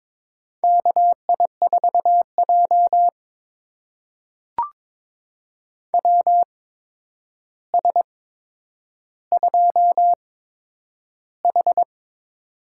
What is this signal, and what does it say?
Morse code, heard as a keyed tone, about 22 wpm, spaced as for 9 wpm, 700 Hz. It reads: KI4J WS2H
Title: USA Call Signs - 4 Characters - 8x Character Spacing - 22wpm